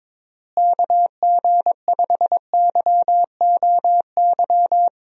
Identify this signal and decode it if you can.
Morse code, heard as a keyed tone, about 23 wpm, 700 Hz. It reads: KG5YOY
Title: USA Call Signs - 6 Characters - Repeated 3x - 22wpm